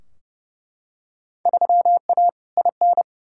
Morse code, heard as a keyed tone, about 30 wpm, 700 Hz. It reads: 3A IN
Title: Field Day Exchanges - 30wpm